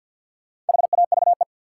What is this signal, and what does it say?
Morse code, heard as a keyed tone, about 47 wpm, 700 Hz. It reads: HAVE
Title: Sets of 2 Words - New Word Drill - 50wpm